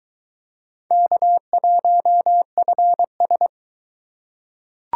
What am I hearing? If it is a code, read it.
Morse code, heard as a keyed tone, about 22 wpm, 700 Hz. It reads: K1FS